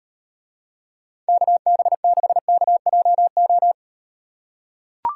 Morse code, heard as a keyed tone, about 38 wpm, 700 Hz. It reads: KB6KJO